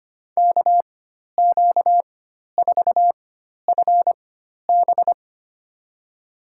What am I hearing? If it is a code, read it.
Morse code, heard as a keyed tone, about 25 wpm, spaced as for 12 wpm, 700 Hz. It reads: KQ4FB
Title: USA Call Signs - 5 Characters - 4x Character Spacing - 25wpm